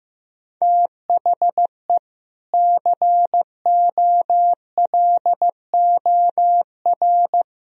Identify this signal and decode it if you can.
Morse code, heard as a keyed tone, about 15 wpm, 700 Hz. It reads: THE COLOR
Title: Sentences from Top 1000 Words - Review Part 2 - 15wpm